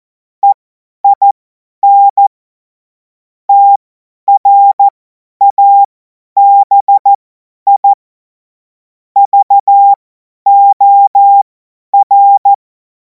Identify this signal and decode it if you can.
Morse code, heard as a keyed tone, about 14 wpm, spaced as for 10 wpm, 800 Hz. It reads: EIN TRABI VOR
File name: Morsekurs / Witze / Witz 21. April 2026 50 F70